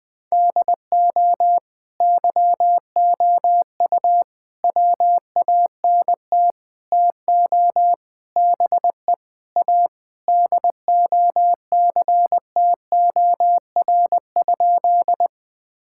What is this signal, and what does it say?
Morse code, heard as a keyed tone, about 20 wpm, 700 Hz. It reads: DO YOU WANT TO BE A DOCTOR?
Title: Sentences with 4-Word Phrases - 20wpm